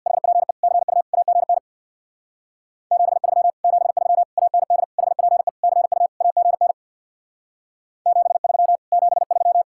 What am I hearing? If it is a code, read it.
Morse code, heard as a keyed tone, about 64 wpm, 700 Hz. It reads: HPE CU AGN 73 73 AND HPE CU AGN 73 73